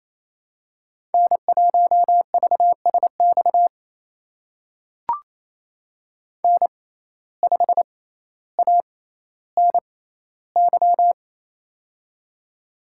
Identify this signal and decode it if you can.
Morse code, heard as a keyed tone, about 28 wpm, spaced as for 14 wpm, 700 Hz. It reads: N1VSX N5ANY
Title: USA Call Signs - 5 Characters - 6x Character Spacing - 28wpm